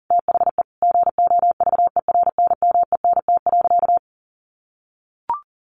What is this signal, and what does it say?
Morse code, heard as a keyed tone, about 40 wpm, 700 Hz. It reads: THE GOVERNMENT.
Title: Sentences from Top 1300 Words 40wpm